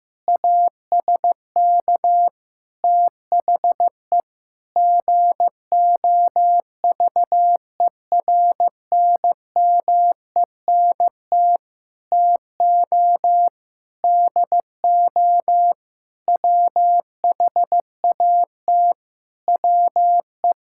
Morse code, heard as a keyed tone, about 16 wpm, 700 Hz. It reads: ASK THE GOVERNMENT TO DO WHAT WE
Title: Sentences from Top 1600 Words 15wpm